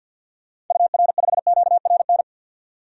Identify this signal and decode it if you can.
Morse code, heard as a keyed tone, about 50 wpm, 700 Hz. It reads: UR5QRN
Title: DXCC-100 Call Signs - 6 Characters - 50wpm